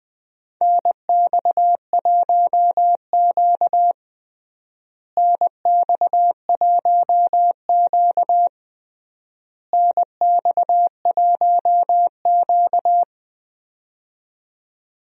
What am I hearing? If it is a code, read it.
Morse code, heard as a keyed tone, about 21 wpm, 700 Hz. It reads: NX1Q NX1Q NX1Q
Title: USA Call Signs - 4 Characters - Repeated 3x - 20wpm